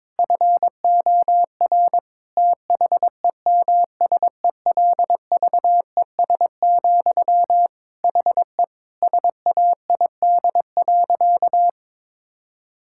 Morse code, heard as a keyed tone, about 22 wpm, 700 Hz. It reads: FOR THEMSELVES, HE SAID.
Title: Sentences from Top 1000 Words - Review Part 6 - 22wpm